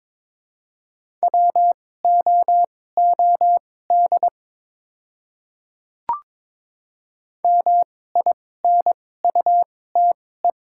Morse code, heard as a keyed tone, about 23 wpm, spaced as for 16 wpm, 700 Hz. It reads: WOOD MINUTE
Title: 2x Character Spacing - Top 500 Words - 22wpm